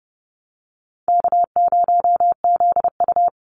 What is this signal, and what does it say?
Morse code, heard as a keyed tone, about 28 wpm, 700 Hz. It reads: K0ZU
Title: Easy Call Signs Repeated 3x - 30wpm